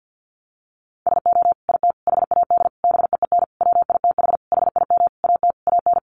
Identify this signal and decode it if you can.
Morse code, heard as a keyed tone, about 50 wpm, 700 Hz. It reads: SO IT HAD BEEN WITH HIM AT AN
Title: Sentences from Top 1200 Words 50wpm